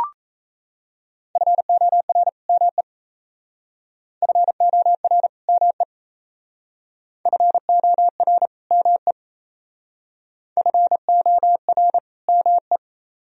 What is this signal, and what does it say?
Morse code, heard as a keyed tone, about 36 wpm, 700 Hz. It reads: FOR ME FOR ME FOR ME FOR ME